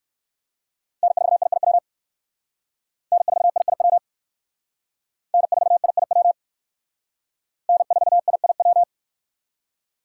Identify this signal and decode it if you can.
Morse code, heard as a keyed tone, about 59 wpm, 700 Hz. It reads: N4IIW N4IIW N4IIW N4IIW